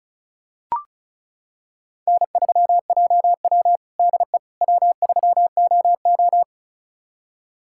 Morse code, heard as a keyed tone, about 34 wpm, 700 Hz. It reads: N3JW DE W3OO
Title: USA Callsign DE Callsign - 4 Characters - 35wpm